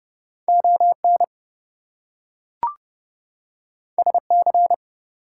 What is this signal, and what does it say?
Morse code, heard as a keyed tone, about 30 wpm, 700 Hz. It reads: ON SC
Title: US States and Canadian Provinces 30wpm